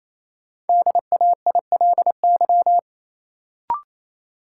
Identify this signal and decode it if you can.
Morse code, heard as a keyed tone, about 28 wpm, 700 Hz. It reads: DAILY
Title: Sentences from Top 1100 Words - New Word Drill - 28wpm